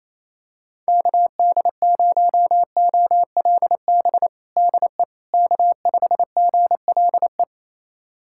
Morse code, heard as a keyed tone, about 28 wpm, 700 Hz. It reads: KD0OLB DE K5GLE
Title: USA Callsign DE Callsign 28wpm